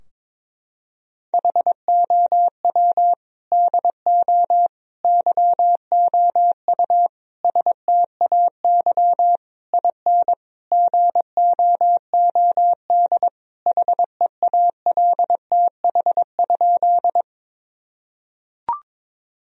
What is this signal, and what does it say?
Morse code, heard as a keyed tone, about 22 wpm, 700 Hz. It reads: HOW DO YOU STAY IN GOOD HEALTH?
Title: Sentences from Top 1000 Words - Review Part 3 - 22wpm